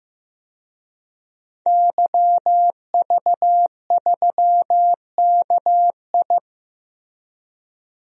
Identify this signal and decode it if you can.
Morse code, heard as a keyed tone, about 15 wpm, 700 Hz. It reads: YV3KI
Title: DXCC-100 Call Signs - 5 Characters - Repeated 2x - 15wpm